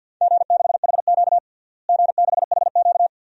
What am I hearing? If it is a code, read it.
Morse code, heard as a keyed tone, about 51 wpm, 700 Hz. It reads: G6HQ G6HQ
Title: DXCC-100 Call Signs - 4 Characters - Repeated 2x - 50wpm